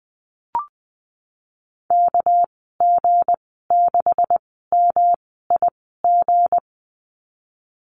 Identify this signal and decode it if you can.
Morse code, heard as a keyed tone, about 20 wpm, spaced as for 15 wpm, 700 Hz. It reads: KG6MIG
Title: 200 Lesson 32g - Letter k - Call Signs 20wpm